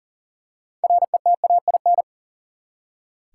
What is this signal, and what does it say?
Morse code, heard as a keyed tone, about 40 wpm, 700 Hz. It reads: RETAIN
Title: Six Letter Words - 40wpm